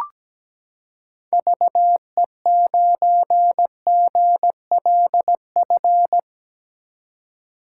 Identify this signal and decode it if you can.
Morse code, heard as a keyed tone, about 17 wpm, 700 Hz. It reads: VE9GLF
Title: K1USN SST QSO Practice - 17wpm